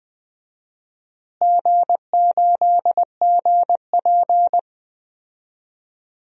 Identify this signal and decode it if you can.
Morse code, heard as a keyed tone, about 20 wpm, 700 Hz. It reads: G8GP
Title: DXCC-100 Call Signs - 4 Characters - Repeated 2x - 20wpm